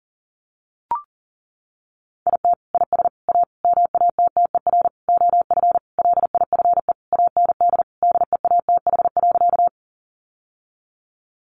Morse code, heard as a keyed tone, about 40 wpm, 700 Hz. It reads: IT IS A MATTER OF LIFE AND DEATH.
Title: Sentences from Top 1100 Words 40wpm